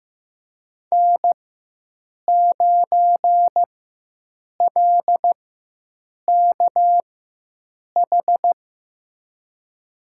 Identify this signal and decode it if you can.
Morse code, heard as a keyed tone, about 15 wpm, spaced as for 7 wpm, 700 Hz. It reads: N9LKH
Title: USA Call Signs - 5 Characters - 4x Character Spacing - 15wpm